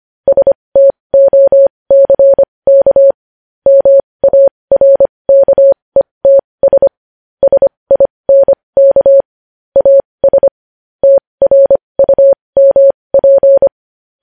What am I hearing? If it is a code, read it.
Morse code, heard as a keyed tone, about 25 wpm, spaced as for 20 wpm, 550 Hz. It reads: STOCK MARKETS SINK AS TRUMP
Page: news-headlines-in-morse-code-at-20-wpm